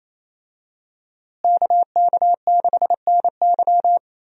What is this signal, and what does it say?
Morse code, heard as a keyed tone, about 28 wpm, 700 Hz. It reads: KK6NY
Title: USA Call Signs - 5 Characters - Repeated 2x - 28wpm